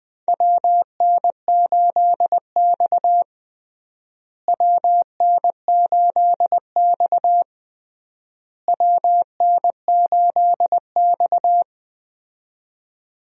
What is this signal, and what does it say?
Morse code, heard as a keyed tone, about 20 wpm, 690 Hz. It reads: WN8X WN8X WN8X
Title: USA Call Signs - 4 Characters - Repeated 3x - 20wpm